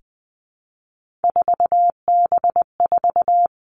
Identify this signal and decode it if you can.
Morse code, heard as a keyed tone, about 20 wpm, 700 Hz. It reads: 4B4
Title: Three Challenging Letter-Number Combinations 20wpm